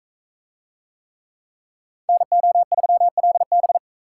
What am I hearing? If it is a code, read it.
Morse code, heard as a keyed tone, about 44 wpm, 700 Hz. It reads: NO3LB